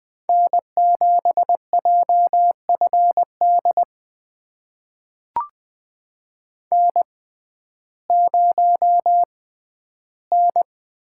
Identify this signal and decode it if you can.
Morse code, heard as a keyed tone, about 20 wpm, spaced as for 12 wpm, 700 Hz. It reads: N7JFD N0N
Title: USA Call Signs - 5 Characters - 6x Character Spacing - 20wpm